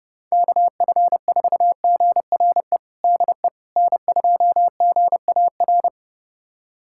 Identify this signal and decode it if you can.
Morse code, heard as a keyed tone, about 30 wpm, 700 Hz. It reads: KF4GRE DE N2GAR
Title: USA Callsign DE Callsign 30wpm